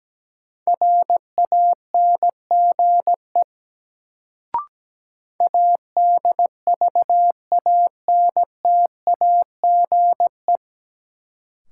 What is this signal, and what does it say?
Morse code, heard as a keyed tone, about 17 wpm, 700 Hz. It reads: RANGE ADVANTAGE
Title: Sentences from Top 900 Words - New Word Drill - 17wpm